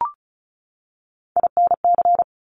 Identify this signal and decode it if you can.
Morse code, heard as a keyed tone, about 36 wpm, 700 Hz. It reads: INC
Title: Top Three Letter Combinations in Words 35wpm